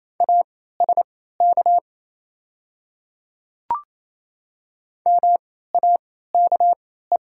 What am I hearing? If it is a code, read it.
Morse code, heard as a keyed tone, about 28 wpm, spaced as for 16 wpm, 700 Hz. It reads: ASK MAKE
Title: 3x Character Spacing - Top 300 Words - 28wpm